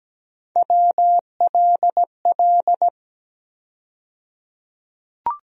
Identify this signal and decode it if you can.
Morse code, heard as a keyed tone, about 17 wpm, 700 Hz. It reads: WLL
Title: USA Call Sign Postfixes - 3 Letters 17wpm